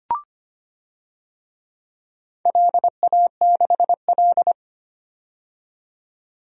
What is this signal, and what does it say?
Morse code, heard as a keyed tone, about 25 wpm, 700 Hz. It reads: LA6L
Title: DXCC-100 Call Signs - 4 Characters - 25wpm